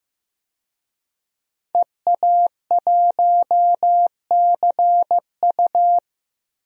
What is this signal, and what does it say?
Morse code, heard as a keyed tone, about 15 wpm, 700 Hz. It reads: EA1CU
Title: DXCC-100 Call Signs - 5 Characters - 15wpm